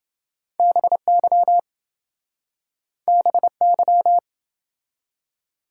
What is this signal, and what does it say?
Morse code, heard as a keyed tone, about 27 wpm, 700 Hz. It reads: BY BY